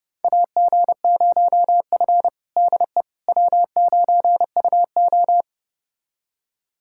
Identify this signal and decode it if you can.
Morse code, heard as a keyed tone, about 31 wpm, 700 Hz. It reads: AG0F DE W9UO
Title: USA Callsign DE Callsign - 4 Characters - 30wpm